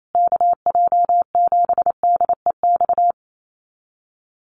Morse code, heard as a keyed tone, about 28 wpm, 700 Hz. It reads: KJ7DEX
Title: USA Call Signs Repeated 3x 28wpm